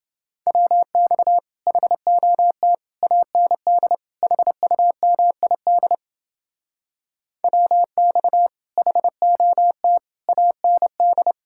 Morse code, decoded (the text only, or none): WX HOT AND HUMID WX HOT AND